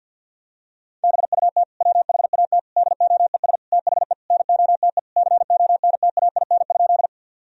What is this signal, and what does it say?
Morse code, heard as a keyed tone, about 50 wpm, 700 Hz. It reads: BUT WHAT DOES THE NOTE CONTAIN?
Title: Sentences from Top 1000 Words - Review Part 2 - 50wpm